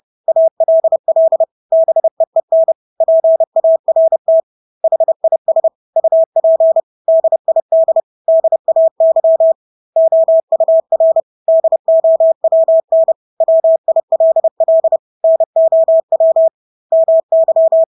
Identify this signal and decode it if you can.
Morse code, heard as a keyed tone, about 30 wpm, 650 Hz. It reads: ALL BEEN PART HIS UP DID DAY OUR DOWN WILL NOW MY
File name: Engelske ord 30-30 wpm